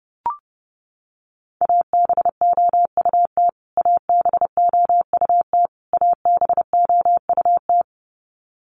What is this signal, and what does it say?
Morse code, heard as a keyed tone, about 30 wpm, 700 Hz. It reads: ABOUT ABOUT ABOUT
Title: Top 500 Words Repeated 30wpm